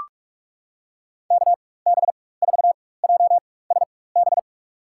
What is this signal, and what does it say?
Morse code, heard as a keyed tone, about 43 wpm, spaced as for 21 wpm, 700 Hz. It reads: KB4JSB
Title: USA Call Signs - 6 Characters - 4x Character Spacing - 45wpm